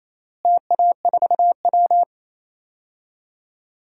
Morse code, heard as a keyed tone, about 27 wpm, 700 Hz. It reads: TA4W
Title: DXCC-100 Call Signs - 4 Characters - Repeated 3x - 28wpm